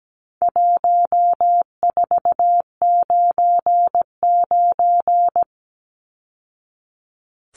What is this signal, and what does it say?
Morse code, heard as a keyed tone, about 17 wpm, 700 Hz. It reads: 1499
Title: Numbers - 4-Digit 17wpm